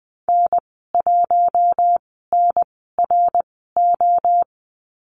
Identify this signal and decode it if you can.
Morse code, heard as a keyed tone, about 20 wpm, 700 Hz. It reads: N 1 N R O
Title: 055 Lesson 10g - Letter r - Call Signs 20wpm